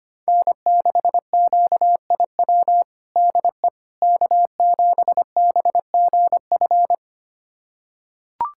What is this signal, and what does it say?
Morse code, heard as a keyed tone, about 25 wpm, 700 Hz. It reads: N6QIW DE K7BGF
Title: USA Callsign DE Callsign - 5 Characters - 25wpm